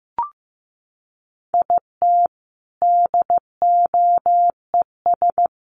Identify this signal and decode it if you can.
Morse code, heard as a keyed tone, about 15 wpm, 700 Hz. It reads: IT DOES
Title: Sentences from Top 1000 Words 15wpm